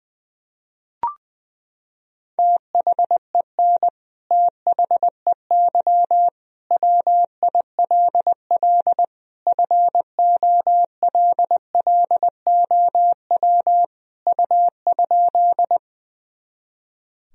Morse code, heard as a keyed tone, about 20 wpm, 700 Hz. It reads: THEN THEY WILL FOLLOW U?
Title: Sentences from Top 200 Words and Abbreviations 20wpm